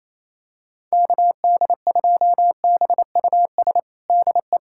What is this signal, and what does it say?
Morse code, heard as a keyed tone, about 28 wpm, 700 Hz. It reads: KD2BUS DE
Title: USA Callsign DE Callsign 28wpm